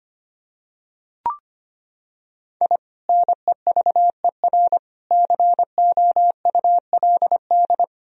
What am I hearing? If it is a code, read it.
Morse code, heard as a keyed tone, about 25 wpm, 700 Hz. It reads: I NEVER COULD
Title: Sentences from Top 1000 Words - Review Part 1 - 25wpm